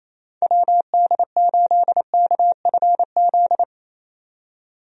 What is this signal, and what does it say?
Morse code, heard as a keyed tone, about 28 wpm, 700 Hz. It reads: WD8KFZ
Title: USA Call Signs - 6 Characters - 28wpm